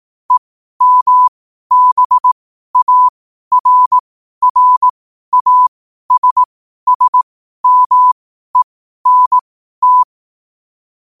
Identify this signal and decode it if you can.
Morse code, heard as a keyed tone, about 18 wpm, spaced as for 12 wpm, 1000 Hz. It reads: EMBARRASSMENT
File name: qotdfiles / 12wpm